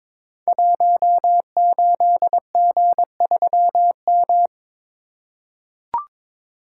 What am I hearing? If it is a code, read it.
Morse code, heard as a keyed tone, about 22 wpm, 700 Hz. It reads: 18G3M
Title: Code Groups - 5 Letter:Numbers - 22wpm